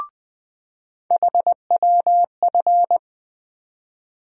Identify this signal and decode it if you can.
Morse code, heard as a keyed tone, about 20 wpm, 700 Hz. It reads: HWF
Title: USA Call Sign Postfixes - 3 Letters 20wpm